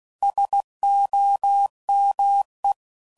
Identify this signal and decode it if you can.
Morse code, heard as a keyed tone, about 16 wpm, 800 Hz. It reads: SOME
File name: CW Code Groups @15WPM / QSOs / WD0BGP Spaced